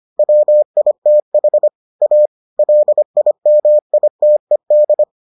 Morse code, heard as a keyed tone, about 25 wpm, 600 Hz. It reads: WITH A LIMITED